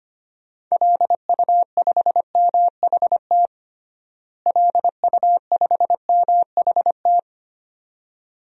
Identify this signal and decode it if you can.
Morse code, heard as a keyed tone, about 25 wpm, 700 Hz. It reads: LU5MHT LU5MHT
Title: DXCC-100 Call Signs - 6 Characters - Repeated 2x - 25wpm